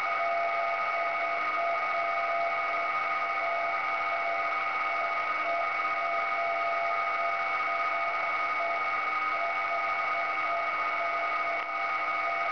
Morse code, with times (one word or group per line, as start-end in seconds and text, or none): none